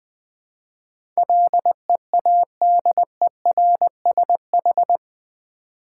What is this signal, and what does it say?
Morse code, heard as a keyed tone, about 20 wpm, 700 Hz. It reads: LEADERSH
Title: Build A Word - 4 Letters At A Time - 10-Letter Words - 20wpm